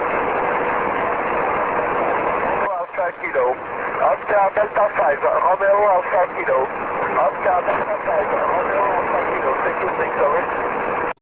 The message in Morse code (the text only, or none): none